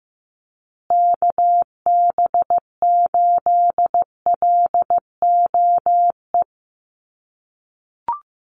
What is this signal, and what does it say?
Morse code, heard as a keyed tone, about 15 wpm, 700 Hz. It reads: KB8LOE
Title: USA Call Signs Repeated 2x 15wpm